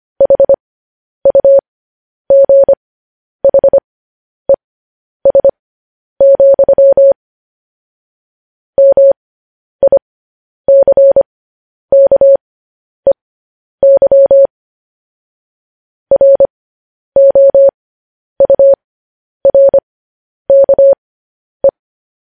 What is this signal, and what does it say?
Morse code, heard as a keyed tone, about 25 wpm, spaced as for 10 wpm, 550 Hz. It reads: HUGHES, MICKEY ROURKE
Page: news headlines in morse code at 10 wpm